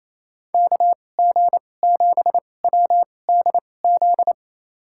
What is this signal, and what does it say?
Morse code, heard as a keyed tone, about 28 wpm, spaced as for 20 wpm, 700 Hz. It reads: KG7WDZ
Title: USA Call Signs - 6 Characters - 2x Character Spacing - 28wpm